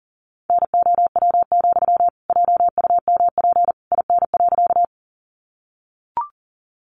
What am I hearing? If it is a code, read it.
Morse code, heard as a keyed tone, about 40 wpm, 700 Hz. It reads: NOW, JUMP IN.
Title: Sentences from Top 1300 Words 40wpm